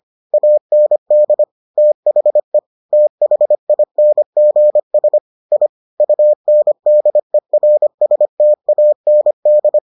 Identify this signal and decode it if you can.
Morse code, heard as a keyed tone, about 25 wpm, 600 Hz. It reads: AND THE THINGS I UNDERSTAND